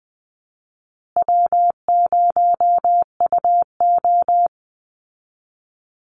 Morse code, heard as a keyed tone, about 20 wpm, 700 Hz. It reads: W0UO
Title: Easy Call Signs Repeated 2x - 20wpm